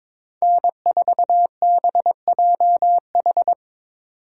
Morse code, heard as a keyed tone, about 26 wpm, 700 Hz. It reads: N4BJH
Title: Speed Racing - USA 1x3 Call Signs 22wpm